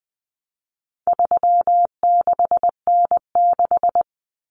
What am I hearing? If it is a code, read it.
Morse code, heard as a keyed tone, about 20 wpm, 700 Hz. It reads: 36N6
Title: Common Cut Numbers - 4-Digit 20wpm